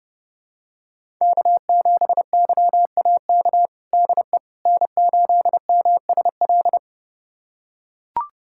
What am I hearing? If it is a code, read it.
Morse code, heard as a keyed tone, about 32 wpm, 700 Hz. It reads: K7YAK DE N8MSL
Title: USA Callsign DE Callsign - 5 Characters - 30wpm